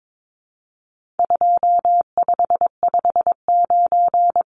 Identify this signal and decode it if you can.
Morse code, heard as a keyed tone, about 22 wpm, 700 Hz. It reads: 2559
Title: Numbers - 4-Digit 22wpm